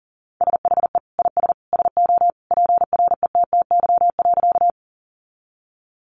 Morse code, heard as a keyed tone, about 37 wpm, 700 Hz. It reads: SHE IS SO PRETTY.
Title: Sentences from Top 1000 Words 40wpm